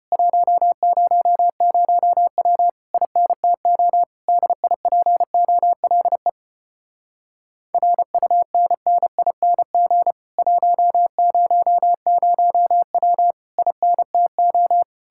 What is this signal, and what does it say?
Morse code, heard as a keyed tone, about 33 wpm, 700 Hz. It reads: 100W INTO DIPOLE RUNNING 100W INTO